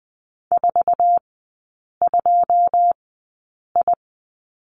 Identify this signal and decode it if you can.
Morse code, heard as a keyed tone, about 20 wpm, spaced as for 8 wpm, 700 Hz. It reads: 42I
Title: 081 Lesson 14e - Number 2 - Three Characters 20wpm